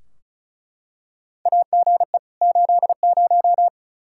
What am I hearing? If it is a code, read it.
Morse code, heard as a keyed tone, about 36 wpm, 700 Hz. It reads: AGE 80
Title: K1USN SST QSO Practice 35wpm